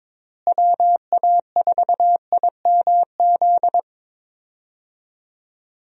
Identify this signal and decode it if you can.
Morse code, heard as a keyed tone, about 22 wpm, 700 Hz. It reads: WA4IMZ
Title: USA Call Signs - 6 Characters - Repeated 3x - 22wpm